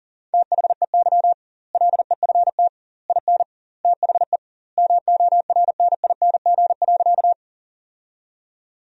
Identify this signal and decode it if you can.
Morse code, heard as a keyed tone, about 40 wpm, 700 Hz. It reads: THEY LEFT IN THE MORNING.
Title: 2x Word-Spacing - Sentences from Top 400 Words - 40wpm